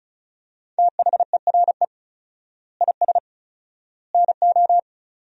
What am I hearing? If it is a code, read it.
Morse code, heard as a keyed tone, about 36 wpm, 700 Hz. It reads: THERE IS NO
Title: 4x Word-Spacing - Sentences from Top 200 Words - 35wpm